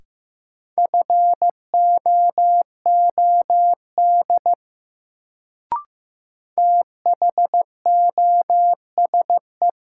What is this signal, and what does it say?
Morse code, heard as a keyed tone, about 15 wpm, 700 Hz. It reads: FOOD THOSE
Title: Sentences from Top 300 Words - New Word Drill - 15wpm